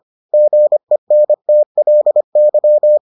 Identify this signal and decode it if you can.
Morse code, heard as a keyed tone, about 25 wpm, 600 Hz. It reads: GENTLY